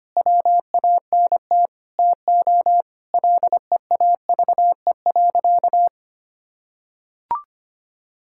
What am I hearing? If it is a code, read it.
Morse code, heard as a keyed tone, about 25 wpm, 700 Hz. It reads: WANT TO LEAVE.
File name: practice / 25wpm / Sentences with 3-Word Phrases - 25wpm